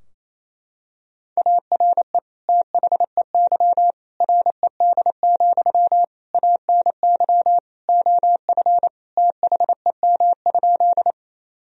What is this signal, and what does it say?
Morse code, heard as a keyed tone, about 28 wpm, 700 Hz. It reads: ARE THEY RED, ANY OF THEM?
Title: Sentences from Top 1000 Words - Review Part 2 - 28wpm